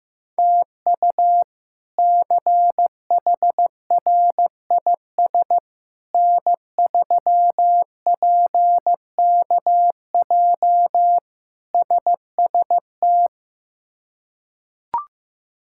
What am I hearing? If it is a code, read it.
Morse code, heard as a keyed tone, about 15 wpm, 700 Hz. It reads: TU CHRIS N3PKJ SST
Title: K1USN SST QSO Practice - 15wpm